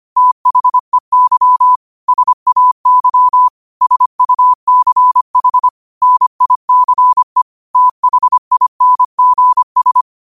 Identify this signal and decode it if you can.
Morse code, heard as a keyed tone, about 25 wpm, 1000 Hz. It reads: THEY SAY SUCH NICE THINGS
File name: 25wpm